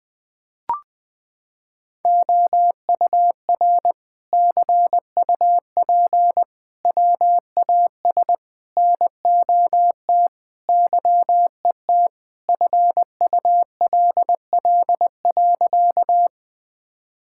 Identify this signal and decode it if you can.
Morse code, heard as a keyed tone, about 21 wpm, 700 Hz. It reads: OUR CUP WAS NOT YET FULL.
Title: Sentences from Top 1600 Words 20wpm